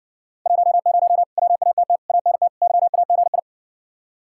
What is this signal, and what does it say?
Morse code, heard as a keyed tone, about 56 wpm, 700 Hz. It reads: 10 WATT ANT YAGI